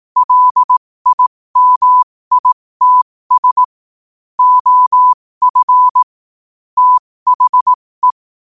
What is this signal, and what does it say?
Morse code, heard as a keyed tone, about 18 wpm, spaced as for 15 wpm, 1000 Hz. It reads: LIMITS OF THE
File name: qotdfiles / 15wpm